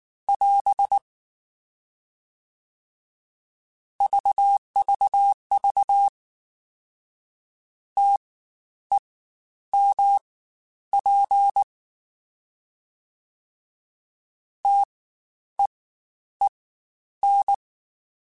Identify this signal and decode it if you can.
Morse code, heard as a keyed tone, about 19 wpm, spaced as for 10 wpm, 800 Hz. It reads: <AS> VVV TEMP TEEN
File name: Spaced Alphas / Spaced Alphas mp3s / Ts sp @18